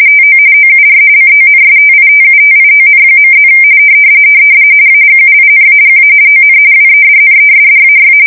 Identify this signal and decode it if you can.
RTTY, about 45.45 baud. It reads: TO OUR CONSTITUTION.+ AGUINAGA ON WEDNESDAY
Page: 170 Shift 2125/2295